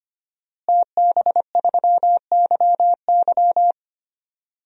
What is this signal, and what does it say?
Morse code, heard as a keyed tone, about 25 wpm, 700 Hz. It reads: TB3YY
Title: DXCC-100 Call Signs - 5 Characters - Repeated 2x - 25wpm